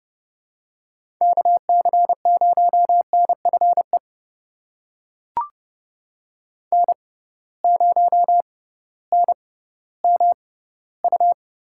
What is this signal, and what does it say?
Morse code, heard as a keyed tone, about 29 wpm, spaced as for 16 wpm, 700 Hz. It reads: KC0NFE N0NMU
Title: USA Call Signs - 6x Character Spacing - 30wpm